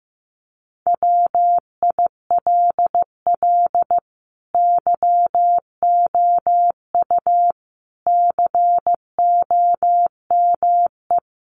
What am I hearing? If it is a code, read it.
Morse code, heard as a keyed tone, about 15 wpm, 700 Hz. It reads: WILL YOU COME
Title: Sentences from Top 100 Words 15wpm